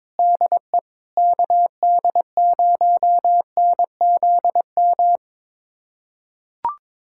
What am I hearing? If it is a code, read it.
Morse code, heard as a keyed tone, about 22 wpm, 700 Hz. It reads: DE KD0NZM
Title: USA Callsign DE Callsign - 6 Characters - 22wpm